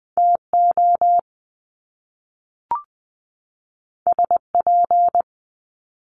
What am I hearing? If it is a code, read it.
Morse code, heard as a keyed tone, about 20 wpm, 700 Hz. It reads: TO SP
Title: Top Two Letter Combinations in Words 20wpm